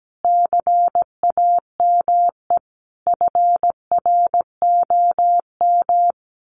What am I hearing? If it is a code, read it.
Morse code, heard as a keyed tone, about 17 wpm, 700 Hz. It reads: CAME FROM
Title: Sentences from Top 1500 Words 17wpm